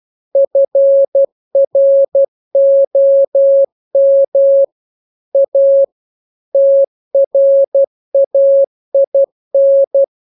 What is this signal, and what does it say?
Morse code, heard as a keyed tone, about 12 wpm, 550 Hz. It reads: FROM A TRAIN